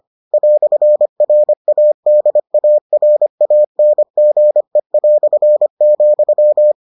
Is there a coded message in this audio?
Yes: Morse code from 0.3 to 6.7 s, heard as a keyed tone, about 25 wpm, 600 Hz. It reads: "RADARANGE",